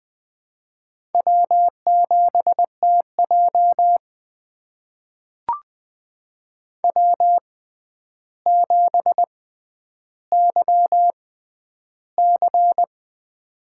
Morse code, heard as a keyed tone, about 20 wpm, spaced as for 10 wpm, 700 Hz. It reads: W7TJ W7YC
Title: USA Call Signs - 4 Characters - 6x Character Spacing - 20wpm